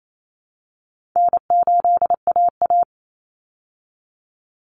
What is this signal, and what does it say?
Morse code, heard as a keyed tone, about 28 wpm, 700 Hz. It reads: N8AA
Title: Call Signs Repeated 2x - 28wpm